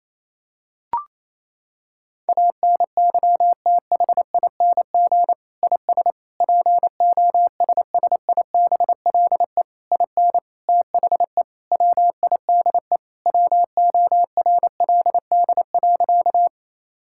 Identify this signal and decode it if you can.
Morse code, heard as a keyed tone, about 28 wpm, 700 Hz. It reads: ANYTHING IS POSSIBLE IN THE WIDE WORLD.
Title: Sentences from Top 1000 Words - Review Part 6 - 28wpm